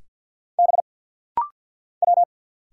Morse code, heard as a keyed tone, about 50 wpm, 700 Hz. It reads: B W
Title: Single Letter-Number - Rapid-Fire 50wpm